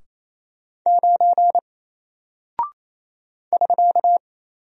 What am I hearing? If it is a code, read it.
Morse code, heard as a keyed tone, about 27 wpm, 700 Hz. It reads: 9 <SK>
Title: Single Character (Letter:Number:Punctuation:Prosign) 28wpm